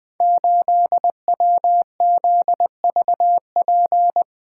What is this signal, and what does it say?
Morse code, heard as a keyed tone, about 20 wpm, 700 Hz. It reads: 8WZVP
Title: Code Groups - 5 Letter:Numbers - 20wpm